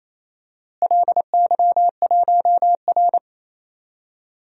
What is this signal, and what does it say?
Morse code, heard as a keyed tone, about 28 wpm, 700 Hz. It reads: LY1R
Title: DXCC-100 Call Signs - 4 Characters - Repeated 2x - 28wpm